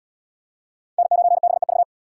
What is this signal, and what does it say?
Morse code, heard as a keyed tone, about 77 wpm, 700 Hz. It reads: N9BEV